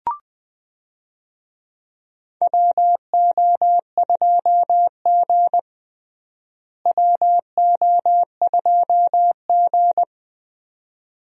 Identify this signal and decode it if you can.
Morse code, heard as a keyed tone, about 20 wpm, 700 Hz. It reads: WO2G WO2G
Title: USA Call Signs - 4 Characters - Repeated 2x - 20wpm